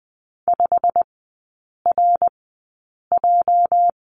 Morse code, heard as a keyed tone, about 20 wpm, spaced as for 8 wpm, 700 Hz. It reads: 5RJ
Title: 205 Lesson 33e - Letter j - Three Characters 20wpm